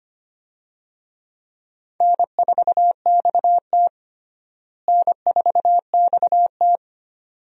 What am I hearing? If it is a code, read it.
Morse code, heard as a keyed tone, about 25 wpm, 700 Hz. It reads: N4XT N4XT
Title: USA Call Signs - 4 Characters - Repeated 2x - 25wpm